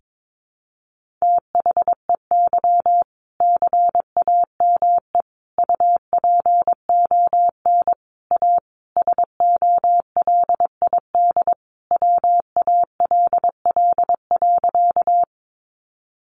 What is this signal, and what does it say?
Morse code, heard as a keyed tone, about 22 wpm, 700 Hz. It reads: THEY CAME UPON A SOLID WALL.
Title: Sentences from Top 1500 Words 22wpm